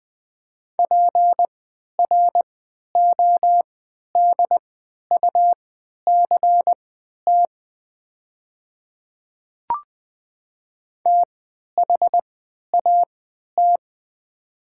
Morse code, heard as a keyed tone, about 20 wpm, spaced as for 11 wpm, 700 Hz. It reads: PRODUCT THAT
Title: 3x Character Spacing - Top 500 Words - 20wpm